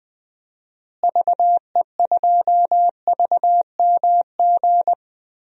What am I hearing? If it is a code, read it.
Morse code, heard as a keyed tone, about 20 wpm, 700 Hz. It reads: VE2VMG
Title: DXCC-100 Call Signs - 6 Characters - Repeated 2x - 20wpm